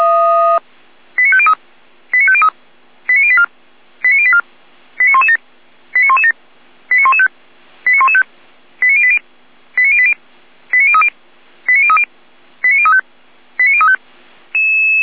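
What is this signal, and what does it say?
Five-tone selective call (ZVEI), heard as quick runs of five tones, tones of about 70 ms. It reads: 89503 89503 89084 89084 89117 89117 89115 89115 89080 89080 89030 89030 89035 89035
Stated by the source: ZVEI five-tone call received by SDR.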